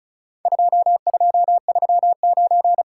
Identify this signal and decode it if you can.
Morse code, heard as a keyed tone, about 34 wpm, 700 Hz. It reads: 2239
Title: Worked All Europe DX Contest - QTC Exchange - 35wpm